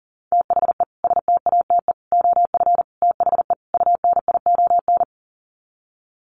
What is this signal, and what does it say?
Morse code, heard as a keyed tone, about 40 wpm, 700 Hz. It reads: THE STATE OF THE UNION
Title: Sets of 5 Words Encore 40wpm